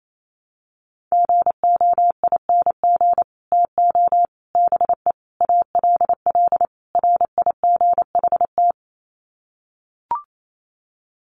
Morse code, heard as a keyed tone, about 28 wpm, 700 Hz. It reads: GOING TO BE ALL RIGHT
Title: Sets of 5 Words Encore2 28wpm